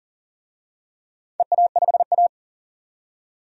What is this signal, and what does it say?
Morse code, heard as a keyed tone, about 40 wpm, 700 Hz. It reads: EA5A
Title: DXCC-100 Call Signs - 4 Characters - 40wpm